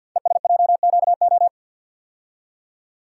Morse code, heard as a keyed tone, about 50 wpm, 700 Hz. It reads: ESJQO
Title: Code Groups - 5 Letters - 50wpm